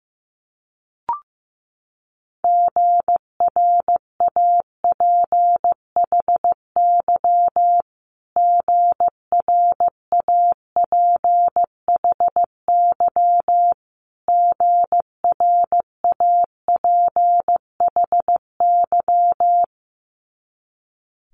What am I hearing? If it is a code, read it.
Morse code, heard as a keyed tone, about 15 wpm, 700 Hz. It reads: GRAPHY GRAPHY GRAPHY
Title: Multi-letter Phonemes Repeated 3x 15wpm